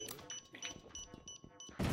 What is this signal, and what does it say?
Speech. There is no code.